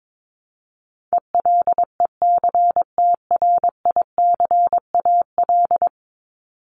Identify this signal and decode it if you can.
Morse code, heard as a keyed tone, about 22 wpm, 700 Hz. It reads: ELECTRICAL
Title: Top 751-1000 Words 22wpm